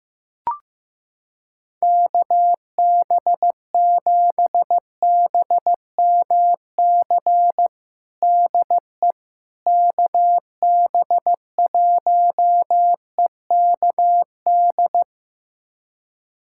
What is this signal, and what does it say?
Morse code, heard as a keyed tone, about 15 wpm, 700 Hz. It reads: KB7BMC DE KB1EKD
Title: USA Callsign DE Callsign - 6 Characters - 15wpm